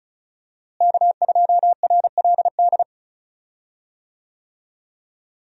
Morse code, heard as a keyed tone, about 34 wpm, 700 Hz. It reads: K2RLD
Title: USA Call Signs - 5 Characters - Repeated 2x - 35wpm